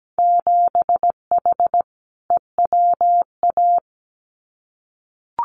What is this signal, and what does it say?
Morse code, heard as a keyed tone, about 17 wpm, 700 Hz. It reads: 7H EWA
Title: Winter Field Day Exchanges Encore 17wpm